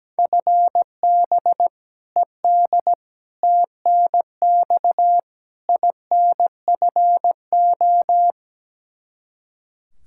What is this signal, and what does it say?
Morse code, heard as a keyed tone, about 17 wpm, 700 Hz. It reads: FB ED TNX INFO
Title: Practice QSO - Exchange 3 - More Info - 17wpm